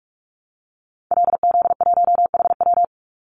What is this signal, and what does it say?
Morse code, heard as a keyed tone, about 44 wpm, 700 Hz. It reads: LZ1HW
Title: Call Signs Repeated 2x - 45wpm